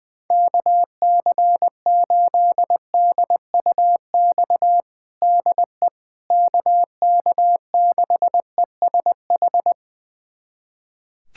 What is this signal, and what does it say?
Morse code, heard as a keyed tone, about 21 wpm, 700 Hz. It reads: KC8DUX DE KK6ESH
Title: USA Callsign DE Callsign 20wpm